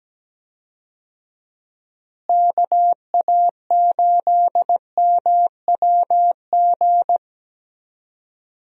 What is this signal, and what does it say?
Morse code, heard as a keyed tone, about 17 wpm, 700 Hz. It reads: KA8MWG
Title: USA Call Signs - 6 Characters - Repeated 2x - 17wpm